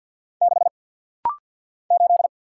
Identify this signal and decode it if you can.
Morse code, heard as a keyed tone, about 50 wpm, 700 Hz. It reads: 6 8
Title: Single Letter-Number - Rapid-Fire 50wpm